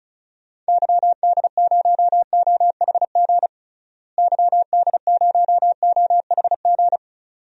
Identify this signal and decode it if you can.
Morse code, heard as a keyed tone, about 36 wpm, 700 Hz. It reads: YD0OHG YD0OHG
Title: DXCC-100 Call Signs - 6 Characters - Repeated 2x - 35wpm